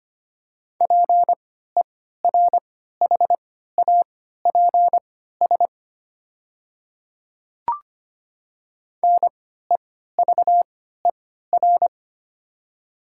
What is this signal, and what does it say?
Morse code, heard as a keyed tone, about 25 wpm, spaced as for 14 wpm, 700 Hz. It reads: PERHAPS NEVER
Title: 3x Character Spacing - Top 500 Words - 25wpm